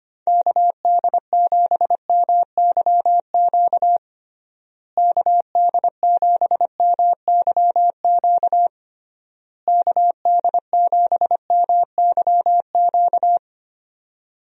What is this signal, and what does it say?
Morse code, heard as a keyed tone, about 25 wpm, 700 Hz. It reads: KD7MYQ KD7MYQ KD7MYQ
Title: USA Call Signs - 6 Characters - Repeated 3x - 25wpm